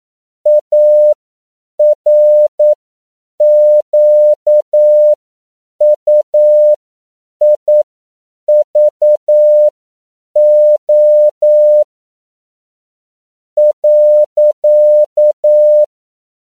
Morse code, heard as a keyed tone, about 9 wpm, spaced as for 7 wpm, 600 Hz. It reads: ARQUIVO .